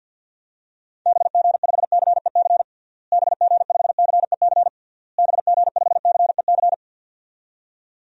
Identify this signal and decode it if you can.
Morse code, heard as a keyed tone, about 51 wpm, 700 Hz. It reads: BG5CEC BG5CEC BG5CEC
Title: DXCC-100 Call Signs - 6 Characters - Repeated 3x - 50wpm